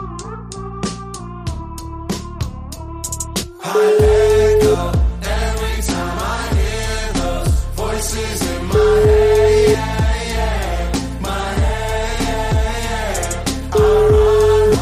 Ringback tone (Europe, 425 Hz), with noise.